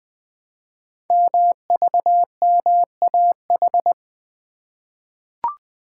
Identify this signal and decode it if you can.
Morse code, heard as a keyed tone, about 20 wpm, 700 Hz. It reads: MVMAH
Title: Code Groups - 5 Letters - 20wpm